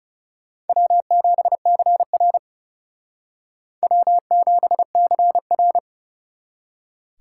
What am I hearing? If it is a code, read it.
Morse code, heard as a keyed tone, about 38 wpm, 700 Hz. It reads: W7CR W7CR